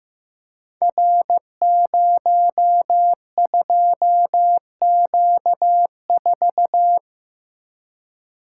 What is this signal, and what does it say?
Morse code, heard as a keyed tone, about 15 wpm, 700 Hz. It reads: R02Q4
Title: Code Groups - 5 Letter:Numbers - 15wpm